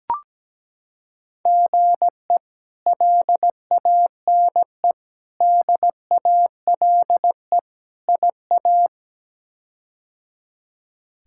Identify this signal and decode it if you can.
Morse code, heard as a keyed tone, about 17 wpm, 700 Hz. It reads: GE LANE DALE IA
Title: K1USN SST QSO Practice - 17wpm